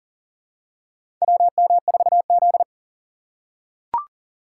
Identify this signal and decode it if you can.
Morse code, heard as a keyed tone, about 42 wpm, 700 Hz. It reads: WM4Z
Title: USA Call Signs - 4 Characters - 6x Character Spacing - 40wpm